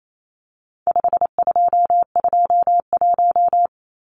Morse code, heard as a keyed tone, about 28 wpm, 700 Hz. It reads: 5221
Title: Numbers - 4-Digit 28wpm